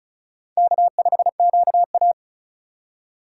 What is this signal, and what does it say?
Morse code, heard as a keyed tone, about 36 wpm, 690 Hz. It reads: K5QA
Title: USA Call Signs - 4 Characters - 3x Character Spacing - 35wpm